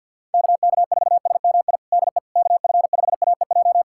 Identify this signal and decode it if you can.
Morse code, heard as a keyed tone, about 47 wpm, 700 Hz. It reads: KK4SMI DE KF5AEJ
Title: USA Callsign DE Callsign - 6 Characters - 50wpm